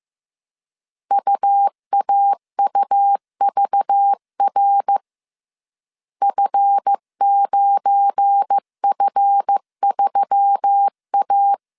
Morse code, heard as a keyed tone, about 15 wpm, 800 Hz. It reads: UAUVR F9F3A
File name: CW / DET / 1 9 1500C S A 14wpm 3222